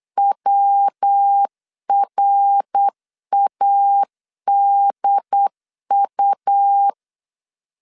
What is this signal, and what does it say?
Morse code, heard as a keyed tone, about 8 wpm, 800 Hz. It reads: WRADU
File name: CW / DET / 1 3 1500C S B 8wpm 4141